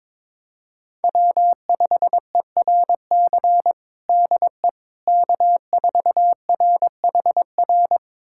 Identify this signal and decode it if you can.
Morse code, heard as a keyed tone, about 21 wpm, 700 Hz. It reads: W5ERC DE K4RHR
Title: USA Callsign DE Callsign 22wpm